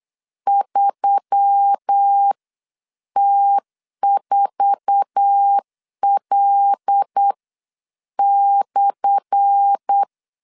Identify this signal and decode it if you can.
Morse code, heard as a keyed tone, about 8 wpm, spaced as for 7 wpm, 800 Hz. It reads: 3T4L/
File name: LEZIONI / 10 SIGNS / 10B INT 8wpm 3534